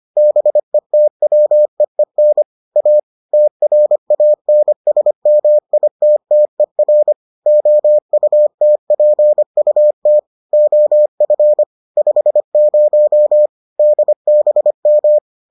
Morse code, heard as a keyed tone, about 25 wpm, 600 Hz. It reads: BETWEEN A TRANSMITTER OUTPUT OF 50 DBM